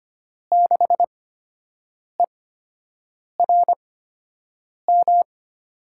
Morse code, heard as a keyed tone, about 25 wpm, 700 Hz. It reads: 6 E R M
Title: USA Call Signs - 8x Character Spacing - 25wpm